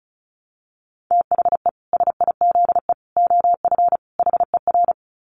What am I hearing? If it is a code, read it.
Morse code, heard as a keyed tone, about 34 wpm, 700 Hz. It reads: THE SIZE OF HER
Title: Dit Challenge 35wpm